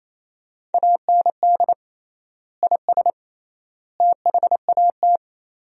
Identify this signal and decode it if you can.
Morse code, heard as a keyed tone, about 28 wpm, 700 Hz. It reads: AND IS THAT
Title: 3x Word-Spacing - Sentences from Top 100 Words - 28wpm